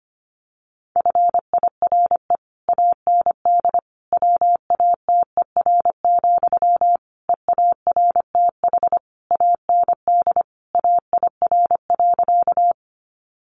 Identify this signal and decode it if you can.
Morse code, heard as a keyed tone, about 25 wpm, 700 Hz. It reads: FIRE AND WATER, EARTH AND AIR.
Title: Sentences from Top 400 Words 25wpm